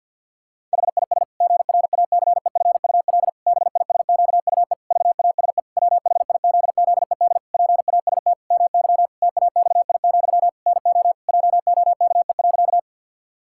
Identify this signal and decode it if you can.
Morse code, heard as a keyed tone, about 50 wpm, 700 Hz. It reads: HIS GRACEFUL BISQUE VASE WHIZZED PAST MY TAXI, NO JOKE.
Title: Pangram Challenge v2 - 50wpm